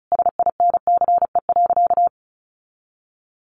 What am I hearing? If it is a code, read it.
Morse code, heard as a keyed tone, about 34 wpm, 700 Hz. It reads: SINCE.
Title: Sentences from Top 1400 Words 35wpm